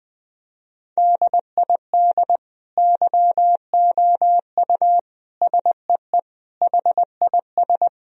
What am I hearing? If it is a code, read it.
Morse code, heard as a keyed tone, about 20 wpm, 700 Hz. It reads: DID YOU SEE HIS
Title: Sentences from Top 1000 Words - Review Part 1 - 20wpm